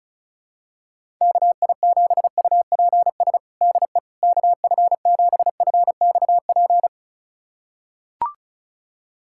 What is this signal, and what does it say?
Morse code, heard as a keyed tone, about 36 wpm, 700 Hz. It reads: KI7UPS DE KF7FXP
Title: USA Callsign DE Callsign - 6 Characters - 35wpm